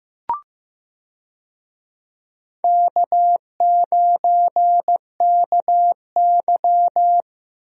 Morse code, heard as a keyed tone, about 15 wpm, 700 Hz. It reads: K9KY
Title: USA Call Signs - 4 Characters - Repeated 2x - 15wpm